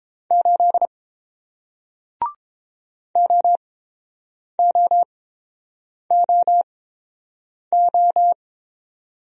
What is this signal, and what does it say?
Morse code, heard as a keyed tone, about 30 wpm, spaced as for 7 wpm, 700 Hz. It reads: 8 OOOO